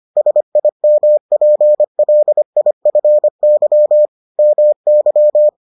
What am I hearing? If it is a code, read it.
Morse code, heard as a keyed tone, about 25 wpm, 600 Hz. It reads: SIMPLIFY MY